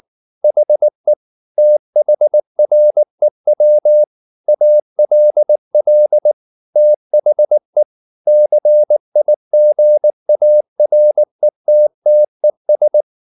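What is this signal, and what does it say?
Morse code, heard as a keyed tone, about 19 wpm, 600 Hz. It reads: HE THREW ALL THE CIGARETTES
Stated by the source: Morse code at 19 wpm from CW med Gnister, reading Ken Follett's novel Never in English.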